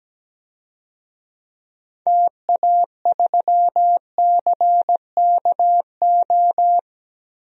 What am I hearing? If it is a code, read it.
Morse code, heard as a keyed tone, about 17 wpm, 700 Hz. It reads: TA3CKO
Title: DXCC-100 Call Signs - 6 Characters - 17wpm